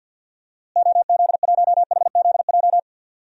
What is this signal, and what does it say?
Morse code, heard as a keyed tone, about 48 wpm, 700 Hz. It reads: OZ1HZJ
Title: DXCC-100 Call Signs - 6 Characters - Repeated 3x - 50wpm